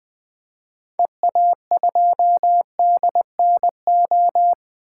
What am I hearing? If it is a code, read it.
Morse code, heard as a keyed tone, about 21 wpm, 700 Hz. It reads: EA2DNO
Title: 2023 SOTA Activator Call Signs - 20wpm